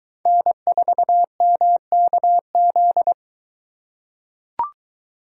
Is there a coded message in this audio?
Yes: Morse code from 0.3 to 3.1 s, heard as a keyed tone, about 22 wpm, 700 Hz. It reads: N4MKZ